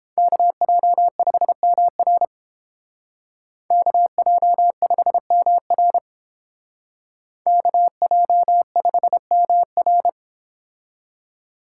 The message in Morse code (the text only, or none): KJ5MR KJ5MR KJ5MR